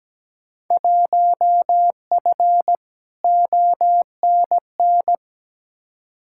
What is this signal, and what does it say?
Morse code, heard as a keyed tone, about 17 wpm, 700 Hz. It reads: 1F ONN
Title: Field Day Exchanges Encore - 17wpm